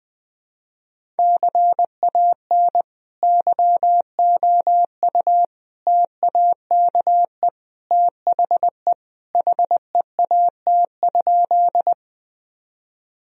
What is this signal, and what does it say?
Morse code, heard as a keyed tone, about 21 wpm, 700 Hz. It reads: CAN YOU TAKE THE HEAT?
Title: Sentences from Top 1000 Words - Review Part 3 - 20wpm